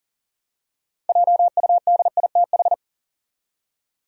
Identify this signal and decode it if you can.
Morse code, heard as a keyed tone, about 39 wpm, 700 Hz. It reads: JUDITH
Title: USA Common Names - 40wpm